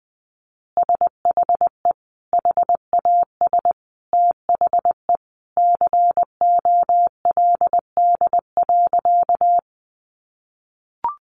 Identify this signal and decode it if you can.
Morse code, heard as a keyed tone, about 20 wpm, 700 Hz. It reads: SHE HAS THE COLD.
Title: Easy Phrases 20wpm